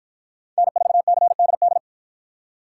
Morse code, heard as a keyed tone, about 48 wpm, 700 Hz. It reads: N4CDD